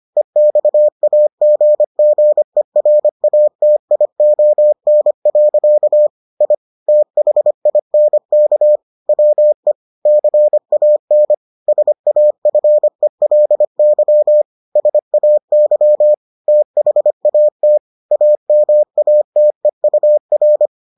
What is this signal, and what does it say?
Morse code, heard as a keyed tone, about 25 wpm, 600 Hz. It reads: EXAGGERATION. I THINK WE CAN SAFELY SAY THAT AMATEUR